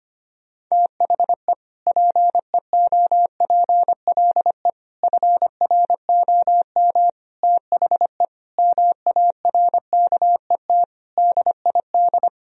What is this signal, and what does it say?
Morse code, heard as a keyed tone, about 25 wpm, 700 Hz. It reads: THE PEOPLE FROM THE MARKET DID
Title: Sentences from Top 1000 Words - Review Part 3 - 25wpm